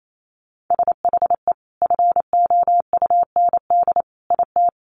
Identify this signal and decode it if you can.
Morse code, heard as a keyed tone, about 28 wpm, 700 Hz. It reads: SHE FOUND IT
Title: Sentences from Top 1500 Words 28wpm